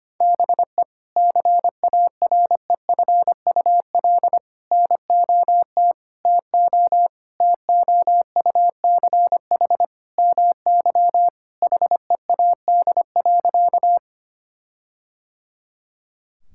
Morse code, heard as a keyed tone, about 25 wpm, 700 Hz. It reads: BE CAREFUL NOT TO TOUCH MY HEAD.
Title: Sentences from Top 1600 Words 25wpm